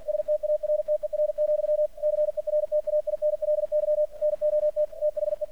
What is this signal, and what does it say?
Morse code, heard as a keyed tone, about 48 wpm, 600 Hz. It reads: STARTER, CERTAINLY NOT THE
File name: archived / CRSnet